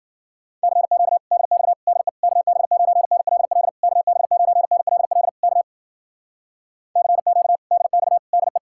Morse code, heard as a keyed tone, about 55 wpm, 700 Hz. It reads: CQ DX DE KB9NLL KB9NLL K CQ DX DE